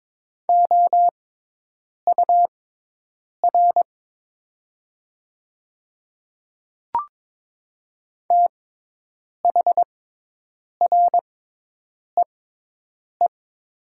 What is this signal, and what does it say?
Morse code, heard as a keyed tone, about 22 wpm, spaced as for 8 wpm, 700 Hz. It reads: OUR THREE
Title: 6x Character Spacing - Top 200 Words - 22wpm